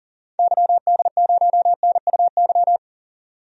Morse code, heard as a keyed tone, about 39 wpm, 700 Hz. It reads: YD0NUY
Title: DXCC-100 Call Signs - 6 Characters - 40wpm